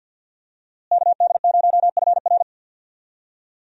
Morse code, heard as a keyed tone, about 51 wpm, 700 Hz. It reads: KD0FR
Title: USA Call Signs - 5 Characters - Repeated 3x - 50wpm